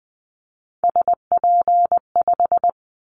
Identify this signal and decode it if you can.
Morse code, heard as a keyed tone, about 20 wpm, 700 Hz. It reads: SP5
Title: Callsign Prefixes for Top 25 DXCC - 20wpm